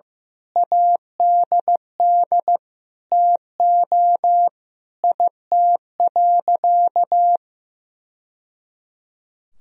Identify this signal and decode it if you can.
Morse code, heard as a keyed tone, about 15 wpm, 700 Hz. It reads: ADD TO IT.
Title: Sentences from Top 1000 Words - Review Part 1 - 15wpm